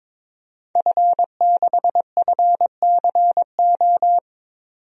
Morse code, heard as a keyed tone, about 22 wpm, 700 Hz. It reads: F6FCO
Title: DXCC-100 Call Signs - 5 Characters - 22wpm